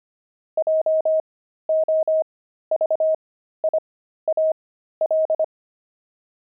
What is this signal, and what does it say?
Morse code, heard as a keyed tone, about 25 wpm, spaced as for 13 wpm, 630 Hz. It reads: JOVIAL
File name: pangram-fs-13-cs-25-friars